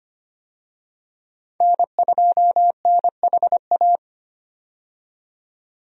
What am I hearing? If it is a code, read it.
Morse code, heard as a keyed tone, about 25 wpm, 700 Hz. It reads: N2NHA
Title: USA Call Signs - 5 Characters - 25wpm